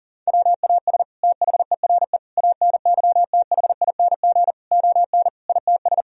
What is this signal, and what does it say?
Morse code, heard as a keyed tone, about 40 wpm, 700 Hz. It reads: WAS THERE ANYTHING ON ITS
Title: Sentences from Top 1000 Words - Review Part 3 - 40wpm